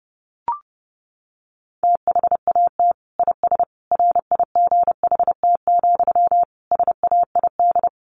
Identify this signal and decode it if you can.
Morse code, heard as a keyed tone, about 30 wpm, 700 Hz. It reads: THAT IS RIGHT, SAID
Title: Sentences from Top 200 Words 30wpm